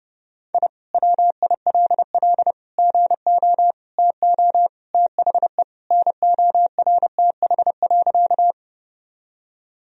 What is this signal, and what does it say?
Morse code, heard as a keyed tone, about 28 wpm, 700 Hz. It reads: I WILL GO TO THE NORTH.
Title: Sentences from Top 1000 Words - Review Part 1 - 30wpm